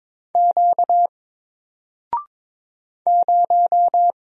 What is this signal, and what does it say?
Morse code, heard as a keyed tone, about 22 wpm, 700 Hz. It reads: Q 0
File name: Single-Letter-Number-22wpm